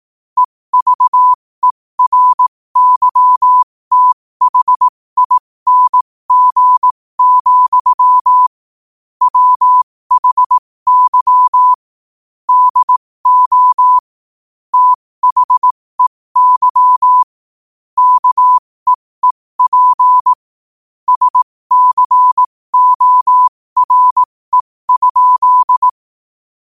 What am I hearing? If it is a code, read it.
Morse code, heard as a keyed tone, about 18 wpm, spaced as for 15 wpm, 1000 Hz. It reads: EVERYTHING, WHY DO THEY KEEP SCORE?
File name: qotdfiles / 15wpm